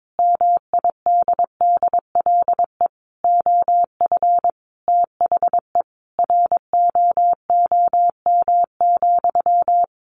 Morse code, heard as a keyed tone, about 22 wpm, 700 Hz. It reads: MIDDLE OF THE ROOM,
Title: Sentences from Top 1400 Words 22wpm